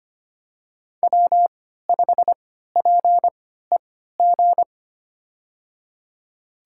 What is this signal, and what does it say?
Morse code, heard as a keyed tone, about 25 wpm, spaced as for 14 wpm, 700 Hz. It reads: W5PEG
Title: USA Call Signs - 5 Characters - 3x Character Spacing - 25wpm